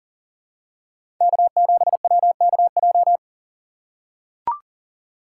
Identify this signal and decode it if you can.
Morse code, heard as a keyed tone, about 39 wpm, 700 Hz. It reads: K7WKJ
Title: USA Call Signs - 2x Character Spacing - 40wpm